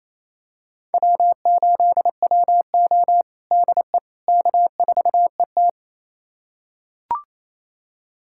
Morse code, heard as a keyed tone, about 28 wpm, 700 Hz. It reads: W8WO DE K4ET
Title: USA Callsign DE Callsign - 4 Characters - 28wpm